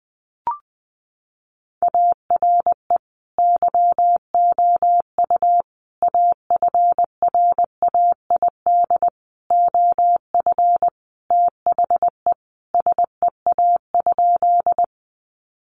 Morse code, heard as a keyed tone, about 20 wpm, 700 Hz. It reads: ARE YOU AFRAID OF THE SEA?
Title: Sentences from Top 1500 Words 20wpm